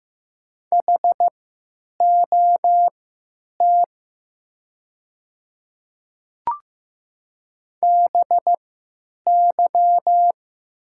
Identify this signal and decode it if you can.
Morse code, heard as a keyed tone, about 15 wpm, spaced as for 9 wpm, 700 Hz. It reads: HOT BY
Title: 3x Character Spacing - Top 200 Words - 15wpm